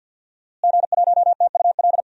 Morse code, heard as a keyed tone, about 52 wpm, 700 Hz. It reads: G1TUL
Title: DXCC-100 Call Signs - 5 Characters - Repeated 3x - 50wpm